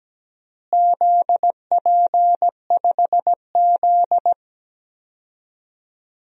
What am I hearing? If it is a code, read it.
Morse code, heard as a keyed tone, about 17 wpm, 700 Hz. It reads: ZP5Z
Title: DXCC-100 Call Signs - 4 Characters - Repeated 3x - 17wpm